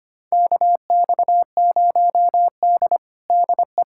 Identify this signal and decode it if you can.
Morse code, heard as a keyed tone, about 25 wpm, 700 Hz. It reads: KX0D DE
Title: USA Callsign DE Callsign - 4 Characters - 25wpm